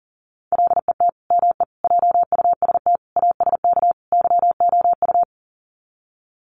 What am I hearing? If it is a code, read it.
Morse code, heard as a keyed tone, about 40 wpm, 700 Hz. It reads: LET ME JUST ASK YOU
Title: Sets of 5 Words Encore2 40wpm